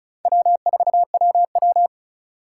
Morse code, heard as a keyed tone, about 36 wpm, 690 Hz. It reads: W4WW
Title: USA Call Signs - 4 Characters - 3x Character Spacing - 35wpm